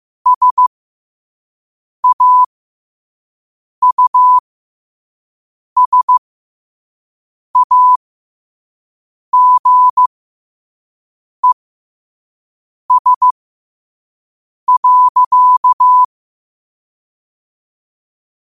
Morse code, heard as a keyed tone, about 15 wpm, spaced as for 5 wpm, 1000 Hz. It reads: SAUSAGES.
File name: qotdfiles / 5wpm